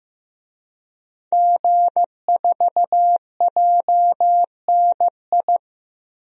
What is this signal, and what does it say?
Morse code, heard as a keyed tone, about 15 wpm, 700 Hz. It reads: G4JNI
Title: DXCC-100 Call Signs - 5 Characters - 15wpm